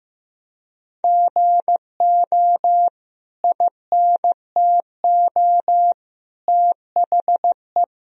Morse code, heard as a keyed tone, about 15 wpm, 700 Hz. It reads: GO INTO THE
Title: Sentences from Top 1000 Words - Review Part 4 - 15wpm